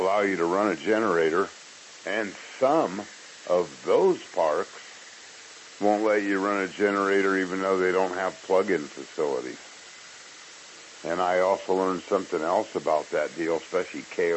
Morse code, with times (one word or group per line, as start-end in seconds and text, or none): none